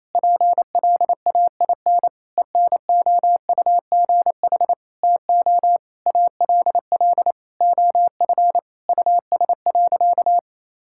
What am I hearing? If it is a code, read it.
Morse code, heard as a keyed tone, about 28 wpm, 700 Hz. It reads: PLAIN ENOUGH TO ALL OF US.
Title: Sentences from Top 1000 Words - Review Part 5 - 28wpm